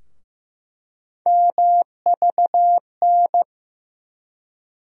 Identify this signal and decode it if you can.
Morse code, heard as a keyed tone, about 15 wpm, 700 Hz. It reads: MVN
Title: USA Call Sign Postfixes - 3 Letters 15wpm